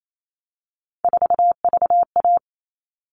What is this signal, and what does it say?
Morse code, heard as a keyed tone, about 28 wpm, 700 Hz. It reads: 4VA
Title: All Cut Numbers - 3-Digit 28wpm